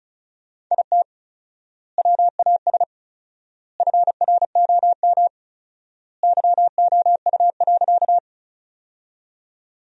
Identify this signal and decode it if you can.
Morse code, heard as a keyed tone, about 36 wpm, 700 Hz. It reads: IT WAS FROM YOU.
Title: 4x Word-Spacing - Sentences from Top 100 Words - 35wpm